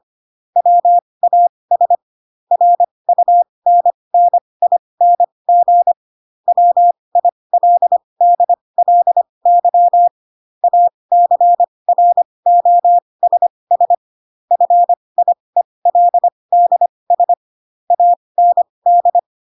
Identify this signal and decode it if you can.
Morse code, heard as a keyed tone, about 25 wpm, spaced as for 20 wpm, 700 Hz. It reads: WAS RUNNING WILDLY ACROSS FIELDS AND
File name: Pinocchio / Pinocchio ch4 15 25 ik1hge